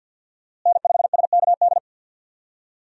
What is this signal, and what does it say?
Morse code, heard as a keyed tone, about 52 wpm, 700 Hz. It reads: N5SKD
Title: USA Call Signs - 5 Characters - Repeated 2x - 50wpm